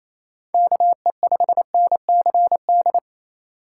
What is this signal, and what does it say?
Morse code, heard as a keyed tone, about 28 wpm, 700 Hz. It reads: KE5NCD
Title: USA Call Signs - 6 Characters - 4x Character Spacing - 28wpm